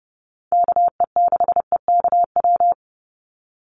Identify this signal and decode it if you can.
Morse code, heard as a keyed tone, about 30 wpm, 700 Hz. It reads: KE6EKW
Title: USA Call Signs 30wpm